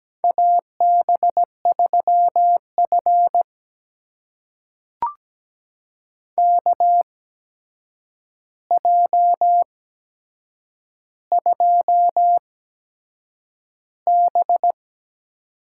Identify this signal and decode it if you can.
Morse code, heard as a keyed tone, about 17 wpm, spaced as for 7 wpm, 700 Hz. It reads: AB3F KJ2B
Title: USA Call Signs - 4 Characters - 8x Character Spacing - 17wpm